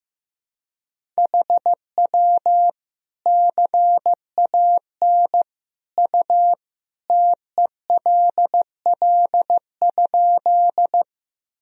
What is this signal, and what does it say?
Morse code, heard as a keyed tone, about 15 wpm, 700 Hz. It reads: HW CAN U TELL?
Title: Sentences from Top 200 Words and Abbreviations 15wpm